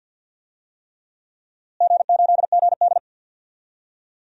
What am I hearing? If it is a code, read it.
Morse code, heard as a keyed tone, about 51 wpm, 700 Hz. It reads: G8GD
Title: DXCC-100 Call Signs - 4 Characters - Repeated 3x - 50wpm